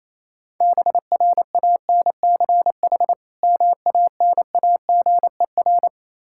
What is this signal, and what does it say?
Morse code, heard as a keyed tone, about 27 wpm, 700 Hz. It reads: BRANCH MANAGER
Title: Practice QSO - Exchange 3 - More Info - 28wpm